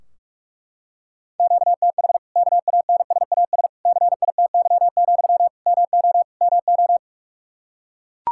Morse code, heard as a keyed tone, about 41 wpm, 700 Hz. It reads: QTH KANSAS CITY, MO MO
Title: Practice QSO - Exchange 1 - Beginning - 45wpm